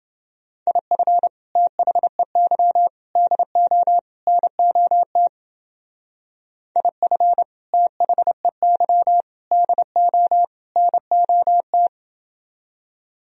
Morse code, heard as a keyed tone, about 29 wpm, 700 Hz. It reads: IF THEY DO NOT IF THEY DO NOT